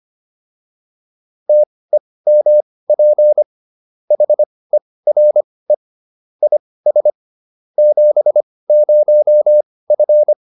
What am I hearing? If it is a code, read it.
Morse code, heard as a keyed tone, about 25 wpm, spaced as for 18 wpm, 600 Hz. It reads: TEMP HERE IS 70F
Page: intermediate-practice-files